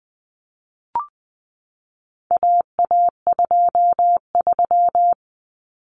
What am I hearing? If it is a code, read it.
Morse code, heard as a keyed tone, about 20 wpm, 700 Hz. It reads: AA23
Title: Common Cut Numbers - 4-Digit 20wpm